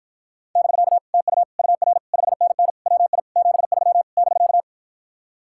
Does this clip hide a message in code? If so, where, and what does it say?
Yes: Morse code from 0.6 to 4.6 s, heard as a keyed tone, about 57 wpm, 700 Hz. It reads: <BK> TU UR 5NN WI 73 <BK>